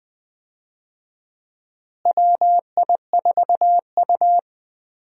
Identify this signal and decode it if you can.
Morse code, heard as a keyed tone, about 20 wpm, 700 Hz. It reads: WI4U
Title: USA Call Signs - 4 Characters - Repeated 3x - 20wpm